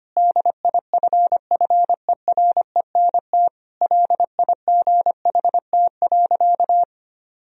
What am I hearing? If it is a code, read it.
Morse code, heard as a keyed tone, about 25 wpm, 700 Hz. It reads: DIFFERENT LIGHT.
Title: Sentences from Top 1000 Words - Review Part 3 - 25wpm